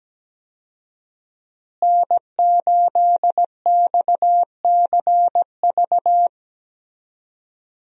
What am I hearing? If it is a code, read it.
Morse code, heard as a keyed tone, about 17 wpm, 700 Hz. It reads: N8XCV
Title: USA Call Signs - 5 Characters - Repeated 3x - 17wpm